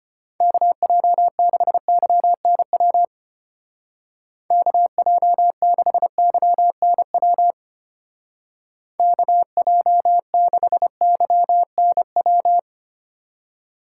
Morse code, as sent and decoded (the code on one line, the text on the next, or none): -.- .--- -.... -.-- -. .-- / -.- .--- -.... -.-- -. .-- / -.- .--- -.... -.-- -. .--
KJ6YNW KJ6YNW KJ6YNW